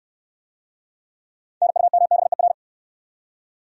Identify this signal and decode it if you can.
Morse code, heard as a keyed tone, about 68 wpm, 700 Hz. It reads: NUMBER